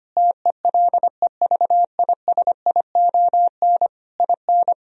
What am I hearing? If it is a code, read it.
Morse code, heard as a keyed tone, about 25 wpm, 700 Hz. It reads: TELEVISION IN
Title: Sentences from Top 1000 Words - Review Part 4 - 25wpm